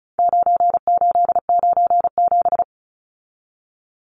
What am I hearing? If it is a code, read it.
Morse code, heard as a keyed tone, about 34 wpm, 700 Hz. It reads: 9897
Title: Numbers - 4-Digit 35wpm